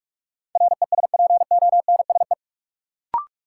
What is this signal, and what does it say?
Morse code, heard as a keyed tone, about 46 wpm, 700 Hz. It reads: RESPONSE
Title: Sentences from Top 1400 Words - New Word Drill - 45wpm